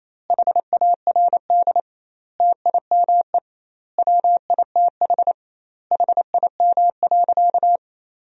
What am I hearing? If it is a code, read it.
Morse code, heard as a keyed tone, about 28 wpm, 700 Hz. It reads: HARD TIME WITH HIM.
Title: 2x Word-Spacing - Sentences from Top 300 Words - 28wpm